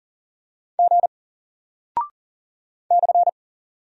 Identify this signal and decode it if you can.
Morse code, heard as a keyed tone, about 40 wpm, 700 Hz. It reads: G /
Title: Single Character (Letter:Number:Punctuation:Prosign) 40wpm